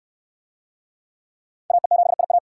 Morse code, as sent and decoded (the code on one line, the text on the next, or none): .- . --... .. .-
AE7IA